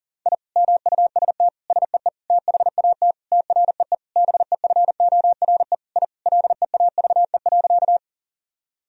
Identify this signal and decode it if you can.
Morse code, heard as a keyed tone, about 40 wpm, 700 Hz. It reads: I MUST SEE THAT TREE BEFORE I LEAVE.
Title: Sentences from Top 1000 Words - Review Part 1 - 40wpm